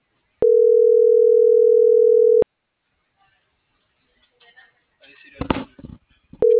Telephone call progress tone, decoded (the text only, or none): ringback tone